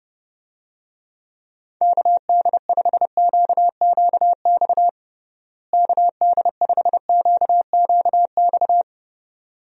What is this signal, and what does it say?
Morse code, heard as a keyed tone, about 32 wpm, 700 Hz. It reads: KD5QQX KD5QQX
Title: USA Call Signs - 6 Characters - Repeated 2x - 30wpm